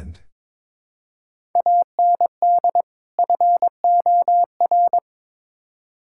Morse code, heard as a keyed tone, about 22 wpm, 700 Hz. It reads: AND FOR
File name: practice / 22wpm / Build A Phrase - 5-Word Phrases - 22wpm